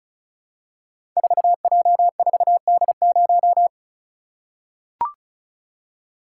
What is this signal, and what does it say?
Morse code, heard as a keyed tone, about 36 wpm, 700 Hz. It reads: 4J4D0
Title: Code Groups - 5 Letter:Numbers - 35wpm